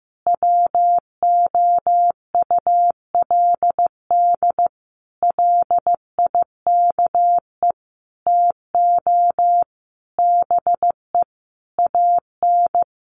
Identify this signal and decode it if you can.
Morse code, heard as a keyed tone, about 15 wpm, 700 Hz. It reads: WOULD LIKE TO BE AN
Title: Sentences from Top 1500 Words 15wpm